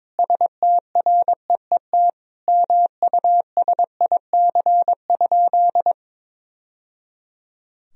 Morse code, heard as a keyed tone, about 22 wpm, 700 Hz. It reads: STREET MUSIC?
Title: Sentences from Top 1000 Words - Review Part 3 - 22wpm